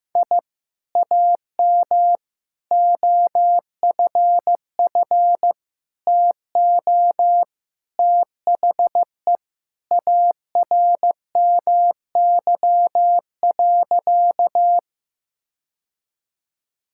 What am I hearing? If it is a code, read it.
Morse code, heard as a keyed tone, about 15 wpm, 700 Hz. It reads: I AM OFF TO THE ARMY.
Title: Sentences from Top 1000 Words - Review Part 5 - 15wpm